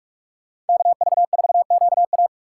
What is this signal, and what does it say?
Morse code, heard as a keyed tone, about 58 wpm, 700 Hz. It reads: KV4QA